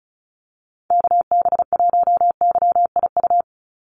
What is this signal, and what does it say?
Morse code, heard as a keyed tone, about 34 wpm, 700 Hz. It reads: KB1YIU
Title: USA Call Signs 35wpm